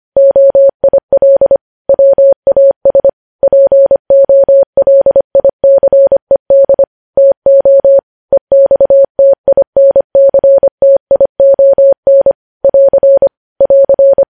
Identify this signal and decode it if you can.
Morse code, heard as a keyed tone, about 25 wpm, 550 Hz. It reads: OIL WAS POLICED TO EXTINCTION <AR> <AR>
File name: podcast / News headlines-1x-25x25